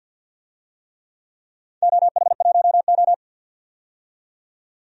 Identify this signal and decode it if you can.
Morse code, heard as a keyed tone, about 51 wpm, 700 Hz. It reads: OH1O
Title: DXCC-100 Call Signs - 4 Characters - 50wpm